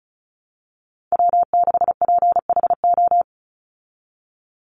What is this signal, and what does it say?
Morse code, heard as a keyed tone, about 34 wpm, 700 Hz. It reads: W6PHO
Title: Call Signs Repeated 3x - 35wpm